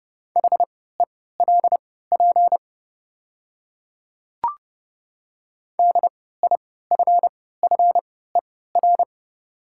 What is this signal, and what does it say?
Morse code, heard as a keyed tone, about 31 wpm, spaced as for 17 wpm, 700 Hz. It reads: HELP DIFFER
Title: 3x Character Spacing - Top 200 Words - 30wpm